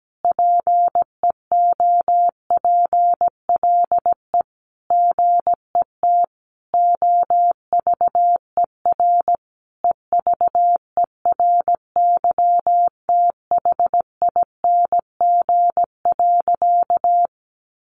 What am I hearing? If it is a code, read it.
Morse code, heard as a keyed tone, about 17 wpm, 700 Hz. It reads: PEOPLE GET OVER EVERYTHING.
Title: Sentences from Top 900 Words 17wpm